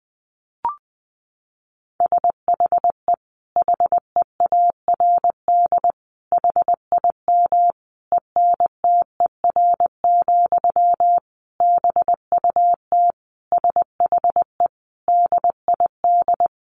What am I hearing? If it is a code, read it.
Morse code, heard as a keyed tone, about 20 wpm, 700 Hz. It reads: SHE HEARD HIM ENTER, BUT SHE DID
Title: Sentences from Top 1400 Words 20wpm